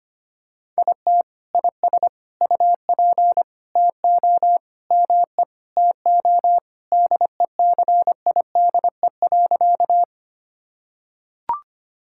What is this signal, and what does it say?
Morse code, heard as a keyed tone, about 25 wpm, 700 Hz. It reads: IT IS UP TO ME TO DECIDE.
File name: practice / 25wpm / Sentences with 2-Word Phrases - 25wpm